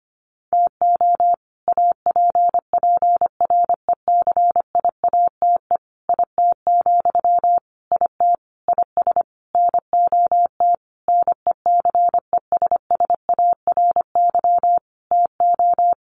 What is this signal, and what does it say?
Morse code, heard as a keyed tone, about 25 wpm, 700 Hz. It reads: TO APPRECIATE IT, IT IS NOT NECESSARY TO
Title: Sentences from Top 1300 Words 25wpm